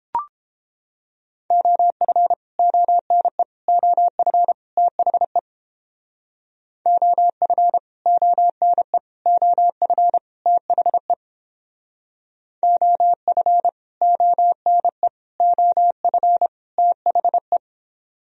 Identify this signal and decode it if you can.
Morse code, heard as a keyed tone, about 29 wpm, 700 Hz. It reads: OF ONE OF THE OF ONE OF THE OF ONE OF THE